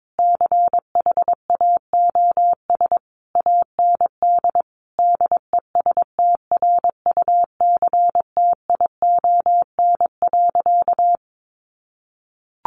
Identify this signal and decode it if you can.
Morse code, heard as a keyed tone, about 22 wpm, 700 Hz. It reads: CHAOS AND DESTRUCTION.
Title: A Sound Of Thunder 22wpm